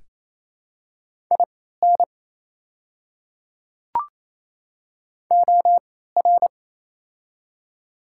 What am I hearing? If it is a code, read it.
Morse code, heard as a keyed tone, about 28 wpm, spaced as for 16 wpm, 700 Hz. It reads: IN OR
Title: 3x Character Spacing - Top 100 Words - 28wpm